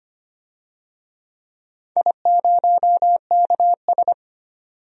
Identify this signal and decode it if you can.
Morse code, heard as a keyed tone, about 25 wpm, 700 Hz. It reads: I0KS
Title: DXCC-100 Call Signs - 4 Characters - 25wpm